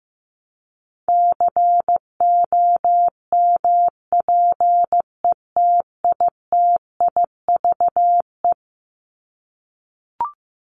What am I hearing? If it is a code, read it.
Morse code, heard as a keyed tone, about 15 wpm, 700 Hz. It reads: COMPETITIVE
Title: Top 1001-1250 Words 15wpm